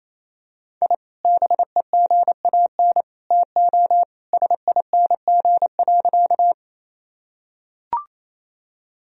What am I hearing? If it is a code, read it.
Morse code, heard as a keyed tone, about 28 wpm, 700 Hz. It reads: I BEGAN TO SING.
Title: Sentences from Top 1000 Words - Review Part 2 - 28wpm